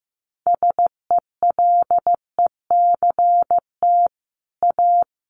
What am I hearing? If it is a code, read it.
Morse code, heard as a keyed tone, about 15 wpm, 700 Hz. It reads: SELECT A
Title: Sentences from Top 1300 Words 15wpm